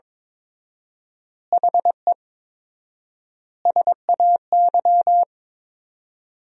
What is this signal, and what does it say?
Morse code, heard as a keyed tone, about 22 wpm, 700 Hz. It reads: HE SAY
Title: 4x Word-Spacing - Sentences from Top 300 Words - 22wpm